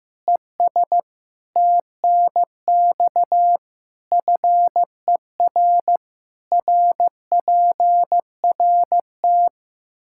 Morse code, heard as a keyed tone, about 15 wpm, 700 Hz. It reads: ES TNX FER RPRT
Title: Practice QSO - Exchange 2 - Info - 15wpm